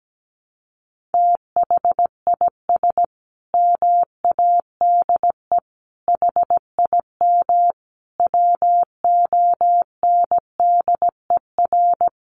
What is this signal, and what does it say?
Morse code, heard as a keyed tone, about 17 wpm, 700 Hz. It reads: THIS MADE HIM WONDER
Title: Sentences from Top 500 Words 17wpm